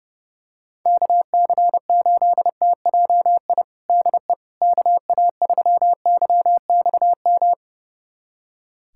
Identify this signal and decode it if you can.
Morse code, heard as a keyed tone, about 30 wpm, 700 Hz. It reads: KC8TJI DE KA3YXM
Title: USA Callsign DE Callsign - 6 Characters - 30wpm